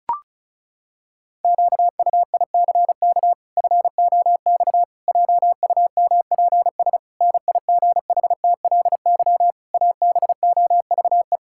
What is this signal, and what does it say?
Morse code, heard as a keyed tone, about 36 wpm, 700 Hz. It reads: QUICK FOX JUMPS NIGHTLY ABOVE
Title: Pangram Challenge v2 - 35wpm